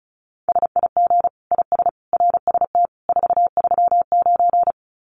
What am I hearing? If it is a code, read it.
Morse code, heard as a keyed tone, about 34 wpm, 700 Hz. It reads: SIG IS RST 439
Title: Common QSO Phrases 35wpm